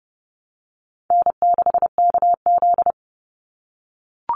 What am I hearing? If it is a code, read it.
Morse code, heard as a keyed tone, about 30 wpm, 700 Hz. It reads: N6KZ
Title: NA SOTA Chaser Callsigns 30wpm